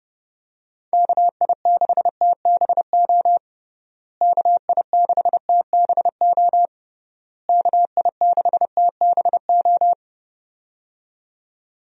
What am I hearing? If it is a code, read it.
Morse code, heard as a keyed tone, about 32 wpm, 700 Hz. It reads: KI6TBO KI6TBO KI6TBO
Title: USA Call Signs - 6 Characters - Repeated 3x - 30wpm